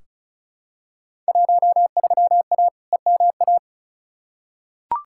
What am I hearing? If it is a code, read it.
Morse code, heard as a keyed tone, about 34 wpm, 700 Hz. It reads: 13A EMA
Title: Field Day Exchanges Encore - 35wpm